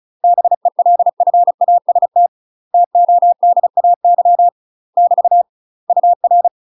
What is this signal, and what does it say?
Morse code, heard as a keyed tone, about 34 wpm, 700 Hz. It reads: BELFAST TODAY <BT> UR
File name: UK MASTER QSOs TRAINING / 35WPM / UK MASTER QSOs 0015